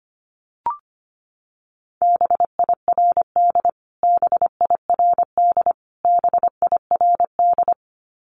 Morse code, heard as a keyed tone, about 25 wpm, 700 Hz. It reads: BIRD BIRD BIRD
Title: Top 500 Words Repeated 25wpm